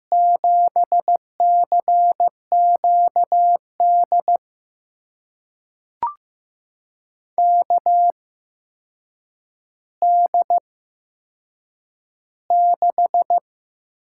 Morse code, heard as a keyed tone, about 15 wpm, spaced as for 7 wpm, 700 Hz. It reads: 7CQD KD6
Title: USA Call Signs - 6 Characters - 8x Character Spacing - 15wpm